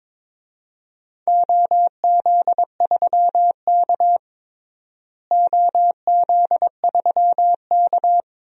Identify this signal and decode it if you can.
Morse code, heard as a keyed tone, about 22 wpm, 700 Hz. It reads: OZ3K OZ3K
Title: DXCC-100 Call Signs - 4 Characters - Repeated 2x - 22wpm